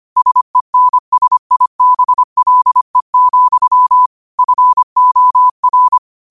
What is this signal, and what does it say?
Morse code, heard as a keyed tone, about 25 wpm, 1000 Hz. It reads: SENSIBLE, FOR
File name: qotdfiles / 25wpm